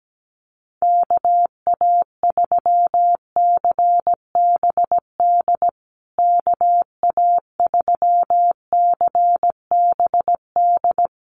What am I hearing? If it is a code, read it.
Morse code, heard as a keyed tone, about 17 wpm, 700 Hz. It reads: KA3CBD KA3CBD
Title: USA Call Signs Repeated 3x 17wpm